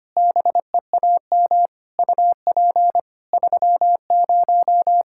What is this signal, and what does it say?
Morse code, heard as a keyed tone, about 25 wpm, 700 Hz. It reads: BEAM UP 30
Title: K1USN SST QSO Practice 25wpm